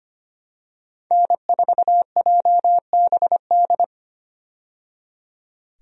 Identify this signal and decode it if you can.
Morse code, heard as a keyed tone, about 25 wpm, 700 Hz. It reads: N4JBD
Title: USA Call Signs - 5 Characters - 25wpm